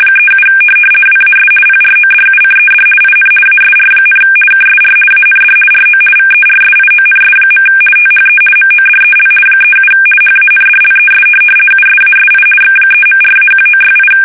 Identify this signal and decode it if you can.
RTTY, about 50 baud. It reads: 37 WAS PRODUCED IN ASR, KSR AND RO VERSIONS. THE MODEL 37 HANDLES USASI X3.4-1968
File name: fsk850-50B / teletype-corp